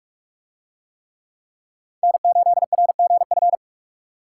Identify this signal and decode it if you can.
Morse code, heard as a keyed tone, about 43 wpm, 700 Hz. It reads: N8RGF